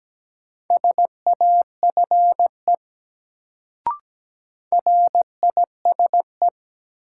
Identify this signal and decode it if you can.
Morse code, heard as a keyed tone, about 17 wpm, 700 Hz. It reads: SAFE RISE
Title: Sentences from Top 900 Words - New Word Drill - 17wpm